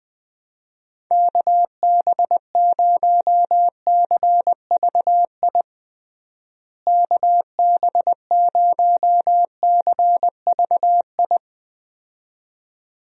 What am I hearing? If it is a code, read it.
Morse code, heard as a keyed tone, about 20 wpm, 700 Hz. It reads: KB0CVI KB0CVI
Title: USA Call Signs - 6 Characters - Repeated 2x - 20wpm